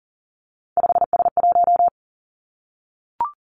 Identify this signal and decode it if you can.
Morse code, heard as a keyed tone, about 37 wpm, 700 Hz. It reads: 5S1
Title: Three Challenging Letter-Number Combinations 40wpm